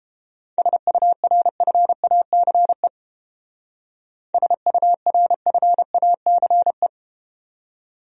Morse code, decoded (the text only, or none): SURFACE SURFACE